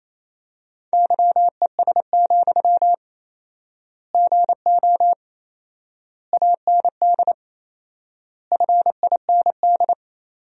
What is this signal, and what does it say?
Morse code, heard as a keyed tone, about 28 wpm, 700 Hz. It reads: YES, GO AND FIND
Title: 4x Word-Spacing - Sentences from Top 500 Words - 28wpm